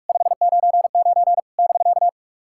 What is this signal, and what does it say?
Morse code, heard as a keyed tone, about 44 wpm, 690 Hz. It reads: 599 <BK>
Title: SOTA QSO Practice 45wpm